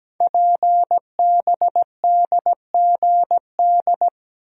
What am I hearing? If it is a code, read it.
Morse code, heard as a keyed tone, about 17 wpm, 700 Hz. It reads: PBDGD
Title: Code Groups - 5 Letters - 17wpm